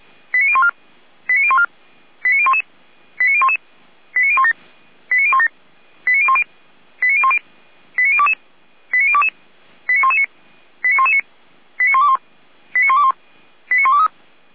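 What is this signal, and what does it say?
Five-tone selective call (ZVEI), heard as quick runs of five tones, tones of about 70 ms. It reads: 89014 89014 89011 89011 89017 89017 89010 89010 89022 89022 89119 89119 89121 89121 89123
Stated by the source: ZVEI five-tone selective call received by SDR.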